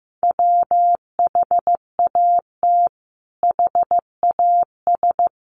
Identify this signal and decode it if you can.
Morse code, heard as a keyed tone, about 15 wpm, 700 Hz. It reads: WHAT HAS
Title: Sentences from Top 600 Words 15wpm